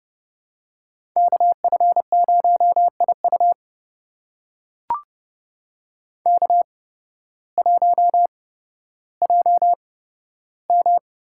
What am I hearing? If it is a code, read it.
Morse code, heard as a keyed tone, about 30 wpm, spaced as for 14 wpm, 700 Hz. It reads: KF0IU K1JM
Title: USA Call Signs - 8x Character Spacing - 30wpm